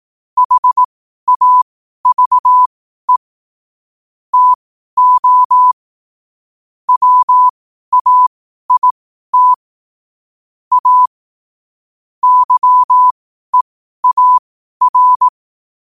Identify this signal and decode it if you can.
Morse code, heard as a keyed tone, about 18 wpm, spaced as for 12 wpm, 1000 Hz. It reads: HAVE TO WAIT A YEAR
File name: qotdfiles / 12wpm